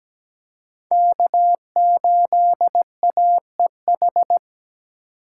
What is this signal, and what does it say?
Morse code, heard as a keyed tone, about 17 wpm, 700 Hz. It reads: K8AEH
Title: USA Call Signs - 5 Characters - 2x Character Spacing - 17wpm